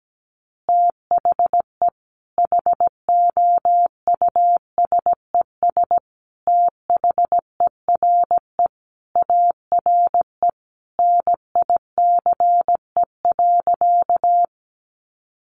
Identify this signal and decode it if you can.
Morse code, heard as a keyed tone, about 17 wpm, 700 Hz. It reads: THE HOUSES THERE ARE NICE.
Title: Easy Phrases 17wpm